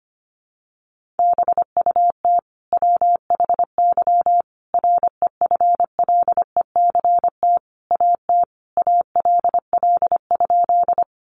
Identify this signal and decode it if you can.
Morse code, heard as a keyed tone, about 25 wpm, 700 Hz. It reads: BUT WHY REFLECT AT ALL?
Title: Sentences from Top 1500 Words 25wpm